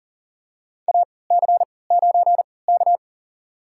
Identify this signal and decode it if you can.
Morse code, heard as a keyed tone, about 40 wpm, spaced as for 23 wpm, 700 Hz. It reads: AC9K
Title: USA Call Signs - 3x Character Spacing - 40wpm